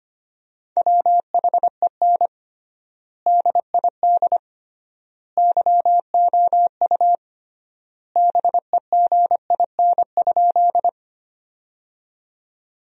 Morse code, heard as a keyed tone, about 25 wpm, 700 Hz. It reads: WHEN DID YOU BEGIN?
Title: 3x Word-Spacing - Sentences from Top 300 Words - 25wpm